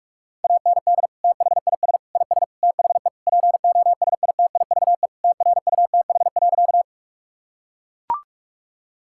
Morse code, heard as a keyed tone, about 41 wpm, 700 Hz. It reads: AND THIS IS THE POSITIVE TRUTH.
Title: Sentences from Top 1000 Words - Review Part 5 - 45wpm